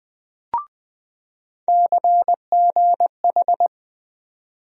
Morse code, heard as a keyed tone, about 21 wpm, 700 Hz. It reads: CGH
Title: USA Call Sign Postfixes - 3 Letters 20wpm